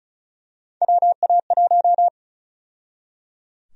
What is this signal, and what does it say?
Morse code, heard as a keyed tone, about 34 wpm, 700 Hz. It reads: WA1
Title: USA Call Sign Prefixes - 35wpm